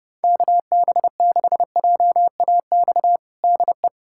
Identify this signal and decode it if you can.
Morse code, heard as a keyed tone, about 32 wpm, 690 Hz. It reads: KB6JAX DE
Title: USA Callsign DE Callsign - 6 Characters - 30wpm